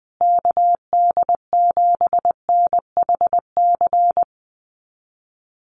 Morse code, heard as a keyed tone, about 19 wpm, 700 Hz. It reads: KD7NHC
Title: USA Call Signs Repeated 3x 20wpm